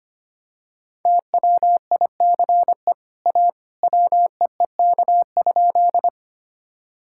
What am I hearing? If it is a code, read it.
Morse code, heard as a keyed tone, about 25 wpm, 700 Hz. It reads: TWICE A WEEK?
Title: Sentences from Top 1600 Words 25wpm